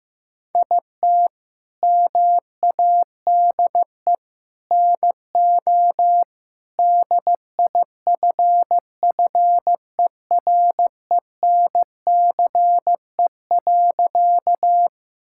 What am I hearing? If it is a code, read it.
Morse code, heard as a keyed tone, about 16 wpm, 700 Hz. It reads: IT MADE NO DIFFERENCE.
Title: Sentences from Top 1000 Words - Review Part 5 - 15wpm